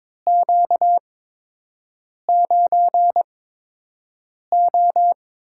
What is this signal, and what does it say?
Morse code, heard as a keyed tone, about 22 wpm, spaced as for 6 wpm, 700 Hz. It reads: Q9O
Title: USA Call Signs - 4 Characters - 8x Character Spacing - 22wpm